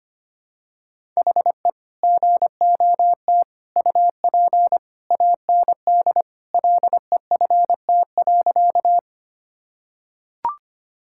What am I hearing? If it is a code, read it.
Morse code, heard as a keyed tone, about 25 wpm, 700 Hz. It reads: HE GOT UP AND LEFT.
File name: practice / 25wpm / Sentences with 2-Word Phrases - 25wpm